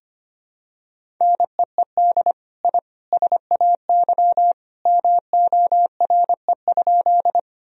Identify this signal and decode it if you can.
Morse code, heard as a keyed tone, about 25 wpm, 700 Hz. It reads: NEED I SAY MORE?
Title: Sentences from Top 1000 Words - Review Part 1 - 25wpm